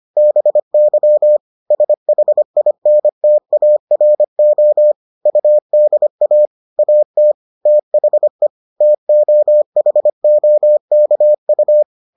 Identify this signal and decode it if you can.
Morse code, heard as a keyed tone, about 25 wpm, 600 Hz. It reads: BY SHINTARO UDA AT THE TOHOKU